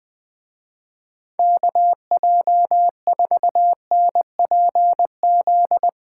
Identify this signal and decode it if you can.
Morse code, heard as a keyed tone, about 20 wpm, 700 Hz. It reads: KJ4NPZ
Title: Speed Racing - USA 2x3 Call Signs 17wpm